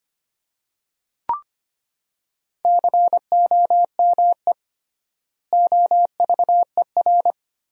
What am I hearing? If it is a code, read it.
Morse code, heard as a keyed tone, about 25 wpm, 700 Hz. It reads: COME OVER
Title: 3x Word-Spacing - Sentences from Top 300 Words - 25wpm